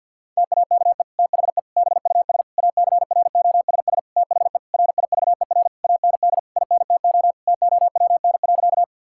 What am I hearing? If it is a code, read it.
Morse code, heard as a keyed tone, about 50 wpm, 700 Hz. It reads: TAKE THE BUS ACROSS THE RIVER AND INTO TOWN.
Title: Sentences from Top 1000 Words - Review Part 6 - 50wpm